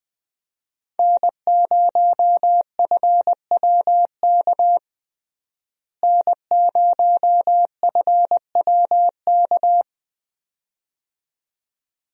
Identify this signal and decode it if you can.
Morse code, heard as a keyed tone, about 20 wpm, 700 Hz. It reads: N0FWK N0FWK
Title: USA Call Signs - 5 Characters - Repeated 2x - 20wpm